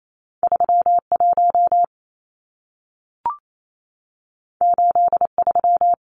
Morse code, heard as a keyed tone, about 28 wpm, 700 Hz. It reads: 31 83
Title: Numbers - 2-Digit 28wpm